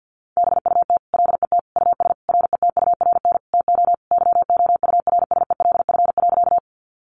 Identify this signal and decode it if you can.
Morse code, heard as a keyed tone, about 50 wpm, 700 Hz. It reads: BUT LET US RETURN TO YOURSELF.
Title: Sentences from Top 800 Words 50wpm